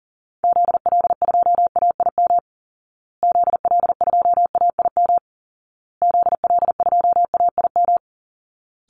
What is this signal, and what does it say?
Morse code, heard as a keyed tone, about 40 wpm, 700 Hz. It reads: ZL2AIM ZL2AIM ZL2AIM
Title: Call Signs Repeated 2x - 40wpm